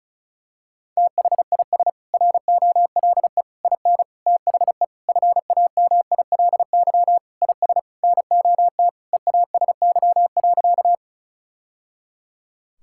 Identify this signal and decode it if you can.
Morse code, heard as a keyed tone, about 34 wpm, 700 Hz. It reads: THIS ROLE IN THE FAMILY IS NOT EASY.
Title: Sentences from Top 1000 Words - Review Part 6 - 35wpm